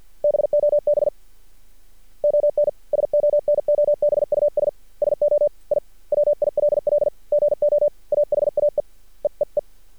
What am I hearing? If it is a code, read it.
Morse code, heard as a keyed tone, about 50 wpm, 590 Hz. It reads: BOB ON SONOBUS SO I WILL GO AHAE EEE
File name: archived / CRSnet